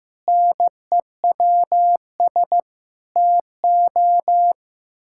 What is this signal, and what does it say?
Morse code, heard as a keyed tone, about 16 wpm, 700 Hz. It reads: NEWS TO
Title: Pangram Challenge v2 - 15wpm